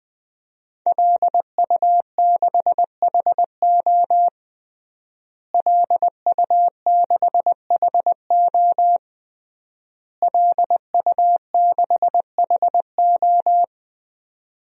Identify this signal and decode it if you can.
Morse code, heard as a keyed tone, about 20 wpm, 700 Hz. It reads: LU6HO LU6HO LU6HO
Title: DXCC-100 Call Signs - 5 Characters - Repeated 3x - 20wpm